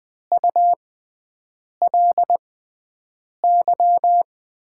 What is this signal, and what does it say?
Morse code, heard as a keyed tone, about 20 wpm, spaced as for 7 wpm, 700 Hz. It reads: ULY
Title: USA Call Signs - 5 Characters - 6x Character Spacing - 20wpm